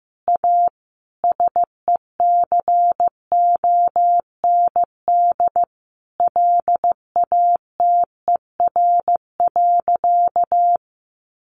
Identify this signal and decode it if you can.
Morse code, heard as a keyed tone, about 15 wpm, 700 Hz. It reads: A SECOND LATER.
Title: Sentences from Top 900 Words 15wpm